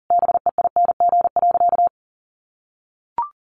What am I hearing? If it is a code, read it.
Morse code, heard as a keyed tone, about 40 wpm, 700 Hz. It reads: BEING.
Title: Sentences from Top 1000 Words 40wpm